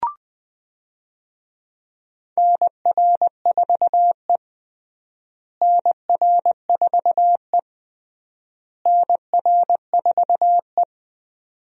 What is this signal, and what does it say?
Morse code, heard as a keyed tone, about 20 wpm, 700 Hz. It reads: NR4E NR4E NR4E
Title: USA Call Signs - 4 Characters - Repeated 3x - 20wpm